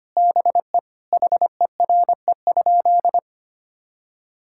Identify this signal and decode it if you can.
Morse code, heard as a keyed tone, about 25 wpm, 700 Hz. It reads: BE HERE?
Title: Sentences from Top 1000 Words - Review Part 5 - 25wpm